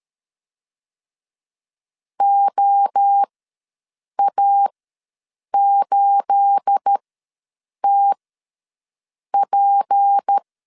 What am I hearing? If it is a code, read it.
Morse code, heard as a keyed tone, about 13 wpm, spaced as for 7 wpm, 800 Hz. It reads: OA8TP
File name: LEZIONI / 9 286 ws 12wpm 3351